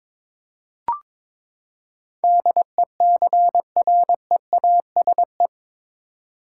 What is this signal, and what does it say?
Morse code, heard as a keyed tone, about 22 wpm, 700 Hz. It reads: DECREASE
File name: practice / 22wpm / Eight Letter Words - 22wpm